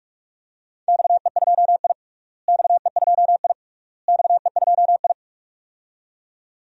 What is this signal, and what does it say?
Morse code, heard as a keyed tone, about 43 wpm, 700 Hz. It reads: XE2I XE2I XE2I
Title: DXCC-100 Call Signs - 4 Characters - Repeated 3x - 45wpm